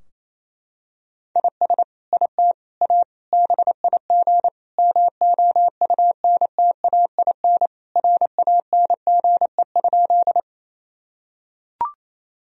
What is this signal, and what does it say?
Morse code, heard as a keyed tone, about 28 wpm, 700 Hz. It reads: IS IT A BIG MOUNTAIN RANGE?
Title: Sentences from Top 1000 Words - Review Part 4 - 28wpm